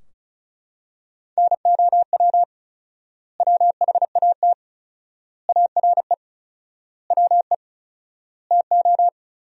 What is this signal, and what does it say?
Morse code, heard as a keyed tone, about 34 wpm, 700 Hz. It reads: NOW WHAT ARE WE TO
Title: 4x Word-Spacing - Sentences from Top 100 Words - 35wpm